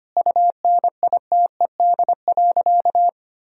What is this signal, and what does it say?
Morse code, heard as a keyed tone, about 25 wpm, 700 Hz. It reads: UNITED.
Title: Sentences from Top 1000 Words - Review Part 4 - 25wpm